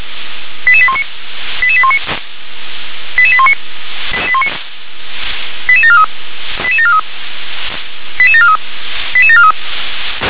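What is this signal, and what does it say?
Five-tone selective call (ZVEI), heard as quick runs of five tones, tones of about 70 ms. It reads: 88819 88819 88819 88743 88743 88743 88743
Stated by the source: ZVEI five-tone selective call received on SDR.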